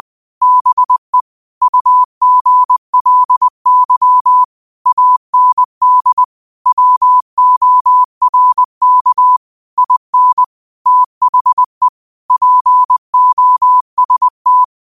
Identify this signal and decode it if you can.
Morse code, heard as a keyed tone, about 20 wpm, 1000 Hz. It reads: BE UGLY AND WORK IN THE POST
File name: qotdfiles / 20wpm